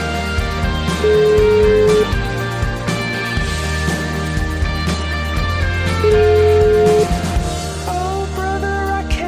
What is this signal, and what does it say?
Ringback tone (Europe, 425 Hz), with noise.